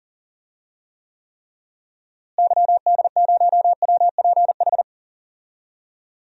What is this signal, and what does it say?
Morse code, heard as a keyed tone, about 40 wpm, 700 Hz. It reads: YD0WPH
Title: DXCC-100 Call Signs - 6 Characters - Repeated 3x - 40wpm